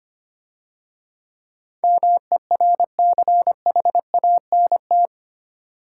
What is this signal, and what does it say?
Morse code, heard as a keyed tone, about 30 wpm, 700 Hz. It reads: MERCHANT
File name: Easy Words - Speed-Racing with Hints - 25wpm